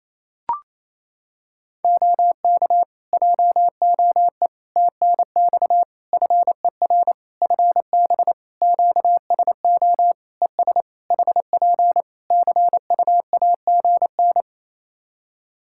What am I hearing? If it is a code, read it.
Morse code, heard as a keyed tone, about 28 wpm, 700 Hz. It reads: OK JOE TNX FER FB QSO ES HP CUAGN
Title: K1USN SST QSO Practice 28wpm